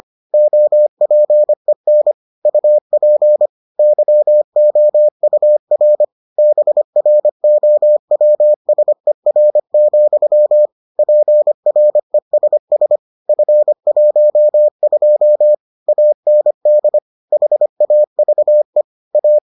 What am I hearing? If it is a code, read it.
Morse code, heard as a keyed tone, about 25 wpm, 600 Hz. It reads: OPEN UP YOUR BROWSER, PRESS F12 AND HAVE A